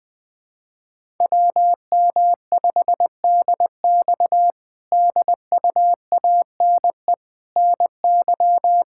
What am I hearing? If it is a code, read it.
Morse code, heard as a keyed tone, about 20 wpm, 700 Hz. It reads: WM5DX DUANE NY
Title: K1USN SST QSO Practice - 20wpm